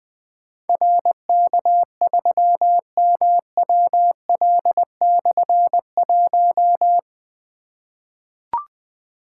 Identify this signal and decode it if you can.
Morse code, heard as a keyed tone, about 20 wpm, 700 Hz. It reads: RK3MWL/1
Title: Difficult Call Signs 20wpm